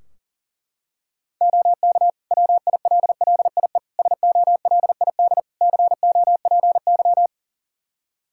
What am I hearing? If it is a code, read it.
Morse code, heard as a keyed tone, about 40 wpm, 700 Hz. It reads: OK WILLIE SOLID COPY
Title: Practice QSO - Exchange 3 - More Info - 40wpm